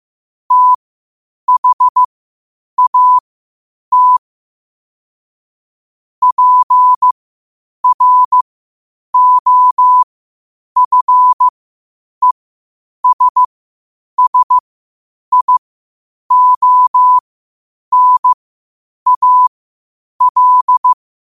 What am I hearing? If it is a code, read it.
Morse code, heard as a keyed tone, about 15 wpm, spaced as for 8 wpm, 1000 Hz. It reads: THAT PROFESSIONAL
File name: qotdfiles / 8wpm